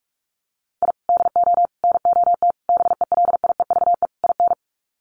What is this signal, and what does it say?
Morse code, heard as a keyed tone, about 46 wpm, 700 Hz. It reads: I DO NOT BELIEVE IN
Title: Sets of 5 Words Encore 45wpm